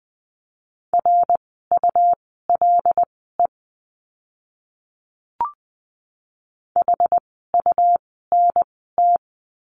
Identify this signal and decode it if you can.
Morse code, heard as a keyed tone, about 20 wpm, spaced as for 14 wpm, 700 Hz. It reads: RULE HUNT
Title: 101 Lesson 17f - Letter u - Words 20wpm